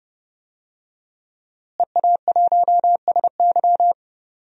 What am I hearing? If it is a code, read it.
Morse code, heard as a keyed tone, about 32 wpm, 700 Hz. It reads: EA1SY
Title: DXCC-100 Call Signs - 5 Characters - Repeated 3x - 30wpm